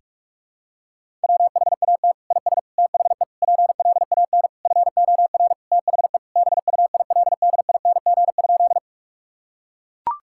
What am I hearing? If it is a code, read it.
Morse code, heard as a keyed tone, about 44 wpm, 700 Hz. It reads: WHAT IS THE PLAN FOR THE BUILDING?
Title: Sentences from Top 1000 Words - Review Part 5 - 45wpm